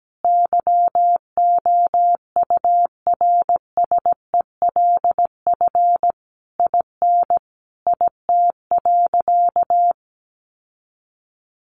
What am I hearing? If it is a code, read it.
Morse code, heard as a keyed tone, about 17 wpm, 700 Hz. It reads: YOURSELF IN IT.
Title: Sentences from Top 1300 Words 17wpm